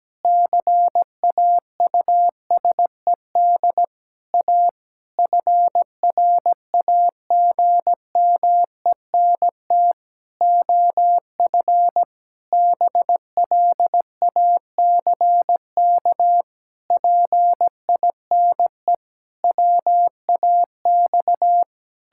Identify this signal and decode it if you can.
Morse code, heard as a keyed tone, about 17 wpm, 700 Hz. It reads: CAUSED A FRAGMENT OF BLACK PINE WAX
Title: Pangram Challenge v2 - 17wpm